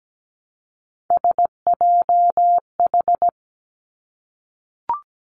Morse code, heard as a keyed tone, about 17 wpm, 700 Hz. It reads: SJH
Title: Three Challenging Letter-Number Combinations 17wpm